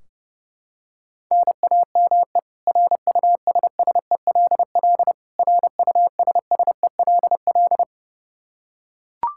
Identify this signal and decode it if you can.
Morse code, heard as a keyed tone, about 32 wpm, 700 Hz. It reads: NAME RUSSELL RUSSELL
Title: K1USN SST QSO Practice 30wpm